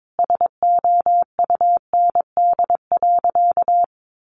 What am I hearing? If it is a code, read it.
Morse code, heard as a keyed tone, about 22 wpm, 700 Hz. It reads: SOUND.
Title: Sentences from Top 1400 Words 22wpm